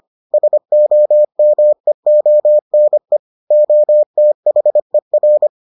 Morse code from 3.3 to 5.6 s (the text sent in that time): OTHER